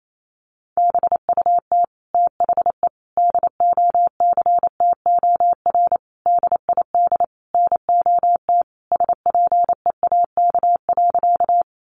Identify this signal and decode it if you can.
Morse code, heard as a keyed tone, about 28 wpm, 700 Hz. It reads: BUT THE DOCTOR DID NOT SPEAK.
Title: Sentences from Top 1500 Words 28wpm